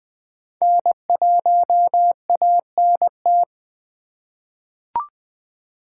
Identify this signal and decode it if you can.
Morse code, heard as a keyed tone, about 20 wpm, 700 Hz. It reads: N1ANT
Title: USA Call Signs - 5 Characters - 2x Character Spacing - 20wpm